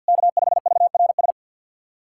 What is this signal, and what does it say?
Morse code, heard as a keyed tone, about 50 wpm, 700 Hz. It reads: K5VRS
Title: SOTA QSO Practice 50wpm